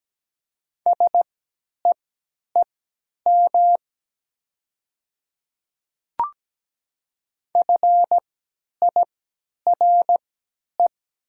Morse code, heard as a keyed tone, about 17 wpm, spaced as for 10 wpm, 700 Hz. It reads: SEEM FIRE
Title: 3x Character Spacing - Top 500 Words - 17wpm